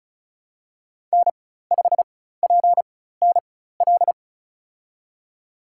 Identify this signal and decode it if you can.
Morse code, heard as a keyed tone, about 34 wpm, spaced as for 16 wpm, 700 Hz. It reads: N5PNL
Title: USA Call Signs - 5 Characters - 4x Character Spacing - 35wpm